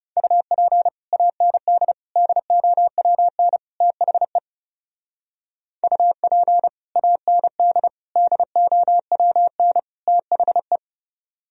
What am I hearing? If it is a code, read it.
Morse code, heard as a keyed tone, about 37 wpm, 700 Hz. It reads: UP AND DOWN THE UP AND DOWN THE
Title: Speed-Racing - Sets of 4 Words 30wpm